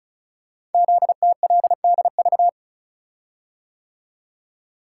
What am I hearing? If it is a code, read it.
Morse code, heard as a keyed tone, about 36 wpm, 700 Hz. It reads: ZTLDV
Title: Code Groups - 5 Letters - 35wpm